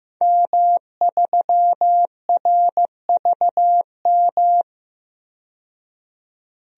Morse code, heard as a keyed tone, about 16 wpm, 700 Hz. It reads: M3RVM
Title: DXCC-100 Call Signs - 5 Characters - 15wpm